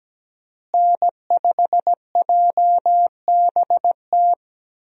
Morse code, heard as a keyed tone, about 17 wpm, 700 Hz. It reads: N5JBT
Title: USA Call Signs - 5 Characters - Repeated 3x - 17wpm